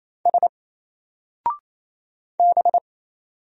Morse code, heard as a keyed tone, about 27 wpm, 700 Hz. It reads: S B
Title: Single Character (Letter:Number:Punctuation:Prosign) 28wpm